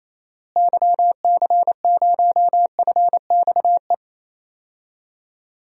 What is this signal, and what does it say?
Morse code, heard as a keyed tone, about 28 wpm, 700 Hz. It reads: YC0FXE
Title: DXCC-100 Call Signs - 6 Characters - Repeated 2x - 28wpm